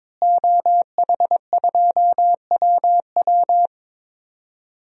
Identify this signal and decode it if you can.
Morse code, heard as a keyed tone, about 21 wpm, 700 Hz. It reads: OH2WW
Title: DXCC-100 Call Signs - 5 Characters - Repeated 3x - 22wpm